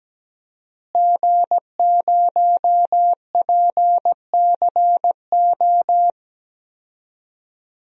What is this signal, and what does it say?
Morse code, heard as a keyed tone, about 17 wpm, 700 Hz. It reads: G0PCO
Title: DXCC-100 Call Signs - 5 Characters - Repeated 3x - 17wpm